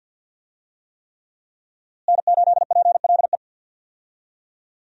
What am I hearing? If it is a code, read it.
Morse code, heard as a keyed tone, about 48 wpm, 700 Hz. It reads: N8PLE
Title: USA Call Signs - 5 Characters - Repeated 3x - 50wpm